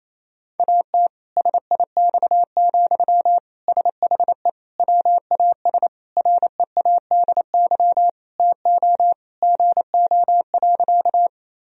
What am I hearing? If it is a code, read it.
Morse code, heard as a keyed tone, about 28 wpm, 700 Hz. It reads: AT SIX, SHE WAS READY TO GO.
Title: Sentences from Top 1000 Words - Review Part 2 - 28wpm